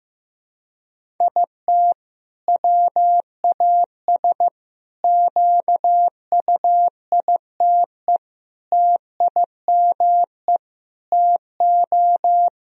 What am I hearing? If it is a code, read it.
Morse code, heard as a keyed tone, about 16 wpm, 700 Hz. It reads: IT WAS QUITE TIME TO
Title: Sentences from Top 1000 Words - Review Part 1 - 15wpm